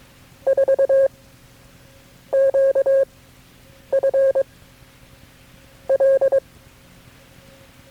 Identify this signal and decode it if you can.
Morse code, heard as a keyed tone, about 22 wpm, 540 Hz. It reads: 4 Q F L
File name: FreeFiles / CW-practice / KB6MT-tape3-side1